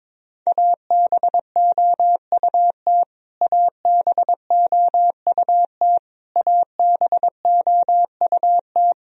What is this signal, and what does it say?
Morse code, heard as a keyed tone, about 22 wpm, 700 Hz. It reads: ABOUT ABOUT ABOUT
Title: Top 100 Words Repeated 22wpm